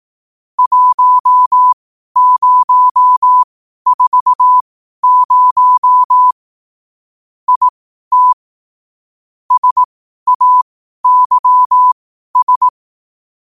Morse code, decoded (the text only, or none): 1040 IT SAYS